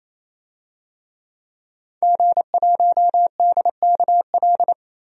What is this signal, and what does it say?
Morse code, heard as a keyed tone, about 27 wpm, 700 Hz. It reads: G1DKL
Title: DXCC-100 Call Signs - 5 Characters - 28wpm